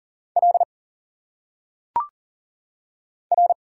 Morse code, heard as a keyed tone, about 40 wpm, 700 Hz. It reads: L R
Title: Single Letters - Phonetic Pronunciation 40wpm